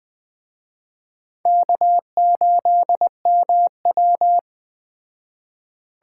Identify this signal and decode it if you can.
Morse code, heard as a keyed tone, about 21 wpm, 700 Hz. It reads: K8MW